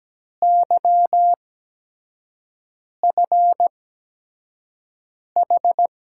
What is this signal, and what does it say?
Morse code, heard as a keyed tone, about 17 wpm, spaced as for 5 wpm, 700 Hz. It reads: YFH
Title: USA Call Signs - 8x Character Spacing - 17wpm